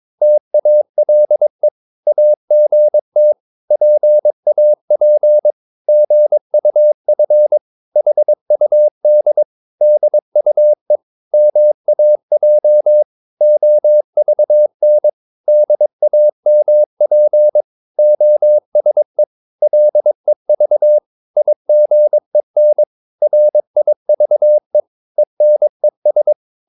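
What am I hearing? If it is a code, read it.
Morse code, heard as a keyed tone, about 22 wpm, 600 Hz. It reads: TALE AGT PAP GUF HUD DUE MAJ OVN DAMP OSE LEV IGEN RIVE ENES
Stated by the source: Morse practice of short Danish words at 22 wpm, from CW med Gnister.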